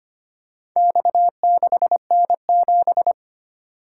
Morse code, heard as a keyed tone, about 25 wpm, 700 Hz. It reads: X6N7
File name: practice / 25wpm / Four Letter-Number Combinations - 25wpm